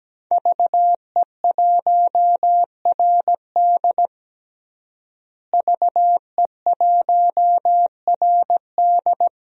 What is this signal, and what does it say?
Morse code, heard as a keyed tone, about 17 wpm, 700 Hz. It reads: VE1RD VE1RD
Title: DXCC-100 Call Signs - 5 Characters - Repeated 3x - 17wpm